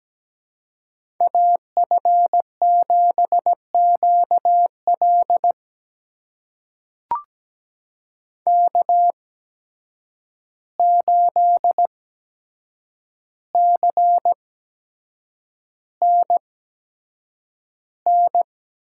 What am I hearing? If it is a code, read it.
Morse code, heard as a keyed tone, about 17 wpm, spaced as for 7 wpm, 700 Hz. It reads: AF7QL K8CNN
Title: USA Call Signs - 5 Characters - 8x Character Spacing - 17wpm